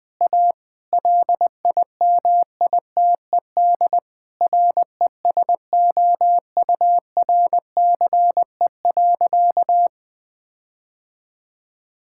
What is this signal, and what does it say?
Morse code, heard as a keyed tone, about 20 wpm, 700 Hz. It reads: A LIMITED RESOURCE.
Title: Difficult Sentences with Hints - 20wpm